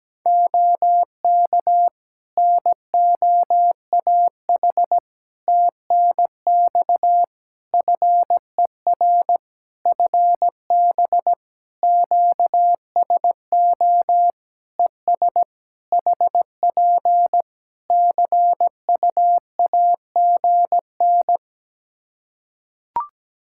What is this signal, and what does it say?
Morse code, heard as a keyed tone, about 17 wpm, 700 Hz. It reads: OK NOAH TNX FER FB QSO ES HP CUAGN
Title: Practice QSO - Exchange 4 - Ending - 17wpm